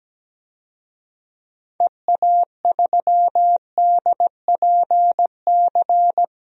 Morse code, heard as a keyed tone, about 17 wpm, 700 Hz. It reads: EA3DPC
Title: DXCC-100 Call Signs - 6 Characters - Repeated 3x - 17wpm